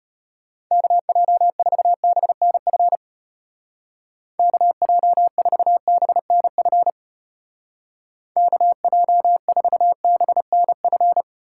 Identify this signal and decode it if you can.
Morse code, heard as a keyed tone, about 32 wpm, 700 Hz. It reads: KJ4BNF KJ4BNF KJ4BNF